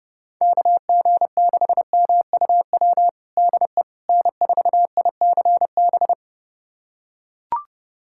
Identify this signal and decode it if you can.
Morse code, heard as a keyed tone, about 31 wpm, 700 Hz. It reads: KG6MUW DE N4ICB
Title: USA Callsign DE Callsign 30wpm